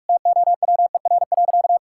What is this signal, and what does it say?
Morse code, heard as a keyed tone, about 46 wpm, 700 Hz. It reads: TOWER.
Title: Pangram Challenge v2 - 45wpm